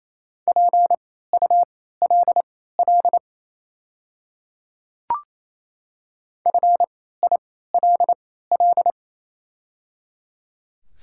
Morse code, heard as a keyed tone, about 28 wpm, spaced as for 16 wpm, 690 Hz. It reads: PULL FILL
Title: 3x Character Spacing - Top 500 Words - 28wpm